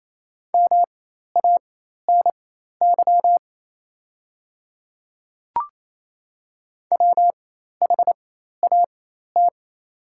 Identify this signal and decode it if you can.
Morse code, heard as a keyed tone, about 28 wpm, spaced as for 13 wpm, 700 Hz. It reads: MANY WHAT
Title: 4x Character Spacing - Top 100 Words - 28wpm